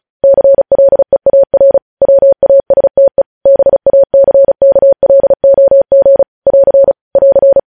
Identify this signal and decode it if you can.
Morse code, heard as a keyed tone, about 36 wpm, 550 Hz. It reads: CLEAR WASTE BACKLOG <AR> <AR>
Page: news headlines in morse code at 35 wpm